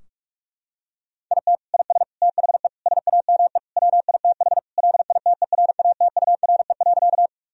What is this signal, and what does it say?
Morse code, heard as a keyed tone, about 41 wpm, 700 Hz. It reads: IT IS THE SAME WITH LITERATURE.
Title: Sentences from Top 1000 Words - Review Part 3 - 45wpm